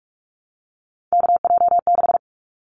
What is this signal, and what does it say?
Morse code, heard as a keyed tone, about 44 wpm, 700 Hz. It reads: KJ6
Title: Callsign Prefixes for Top 25 DXCC - Repeated 3x - 45wpm